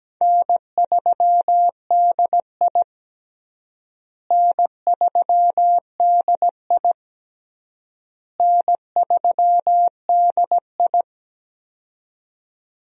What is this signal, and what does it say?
Morse code, heard as a keyed tone, about 17 wpm, 700 Hz. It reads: N3DI N3DI N3DI
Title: USA Call Signs - 4 Characters - Repeated 3x - 17wpm